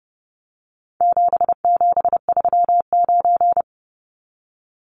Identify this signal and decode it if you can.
Morse code, heard as a keyed tone, about 30 wpm, 700 Hz. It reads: 7739
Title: Numbers - 4-Digit 30wpm